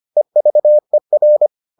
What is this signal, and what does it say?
Morse code, heard as a keyed tone, about 25 wpm, 600 Hz. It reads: EVER